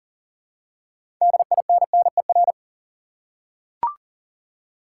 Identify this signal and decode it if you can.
Morse code, heard as a keyed tone, about 40 wpm, 700 Hz. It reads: DINNER
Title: Six Letter Words - 40wpm